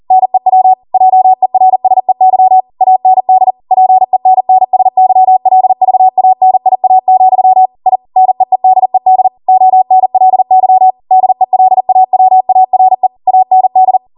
Morse code, heard as a keyed tone, about 40 wpm, 750 Hz. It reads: NEW JERSEY AND PENNSYLVANIA, I NEEDED ONLY DELAWARE AND